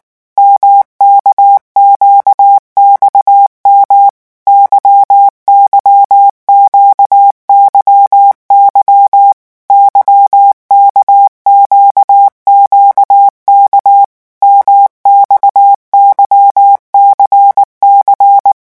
Morse code, heard as a keyed tone, about 19 wpm, 800 Hz. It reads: MKQXM YYQYY YKQQK MXYCC
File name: revision1-vitesse 18 mots